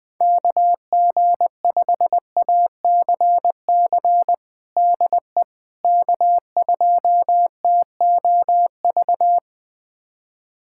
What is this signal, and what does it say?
Morse code, heard as a keyed tone, about 20 wpm, 700 Hz. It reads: KG5ACC DE K2TOV
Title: USA Callsign DE Callsign 20wpm